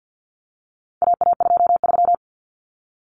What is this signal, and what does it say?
Morse code, heard as a keyed tone, about 48 wpm, 700 Hz. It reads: AA23
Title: Common Cut Numbers - 4-Digit 50wpm